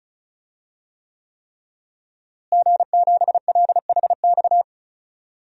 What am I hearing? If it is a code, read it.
Morse code, heard as a keyed tone, about 34 wpm, 700 Hz. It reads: G7LHX
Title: DXCC-100 Call Signs - 5 Characters - Repeated 3x - 35wpm